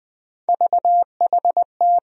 Morse code, heard as a keyed tone, about 20 wpm, 700 Hz. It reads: VHT
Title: USA Call Sign Postfixes - 3 Letters 20wpm